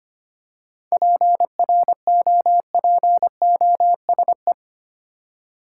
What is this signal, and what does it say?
Morse code, heard as a keyed tone, about 25 wpm, 700 Hz. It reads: PROPOSE
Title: Seven Letter Words - 25wpm